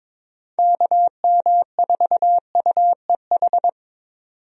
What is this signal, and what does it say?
Morse code, heard as a keyed tone, about 22 wpm, 700 Hz. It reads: KM4UEH
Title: USA Call Signs - 6 Characters - 22wpm